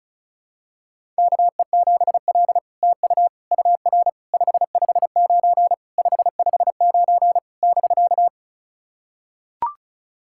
Morse code, heard as a keyed tone, about 36 wpm, 700 Hz. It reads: KE7L TU UR 559 559 <BK>
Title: SOTA QSO Practice 35wpm